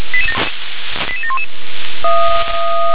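Five-tone selective call (ZVEI), heard as quick runs of five tones, tones of about 70 ms. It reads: 88711
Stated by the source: ZVEI five-tone selective call received on SDR.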